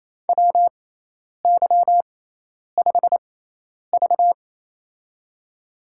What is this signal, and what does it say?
Morse code, heard as a keyed tone, about 28 wpm, spaced as for 10 wpm, 700 Hz. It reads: WY5V
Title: USA Call Signs - 4 Characters - 6x Character Spacing - 28wpm